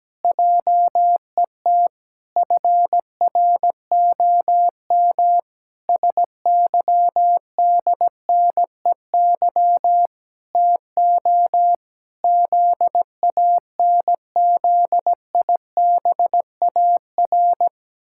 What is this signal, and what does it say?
Morse code, heard as a keyed tone, about 17 wpm, 700 Hz. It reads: JET FROM SYDNEY TO ZANZIBAR
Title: Pangram Challenge v2 - 17wpm